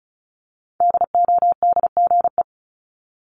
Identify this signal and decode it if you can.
Morse code, heard as a keyed tone, about 36 wpm, 700 Hz. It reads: DODGE
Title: Multi-letter Phonemes 35wpm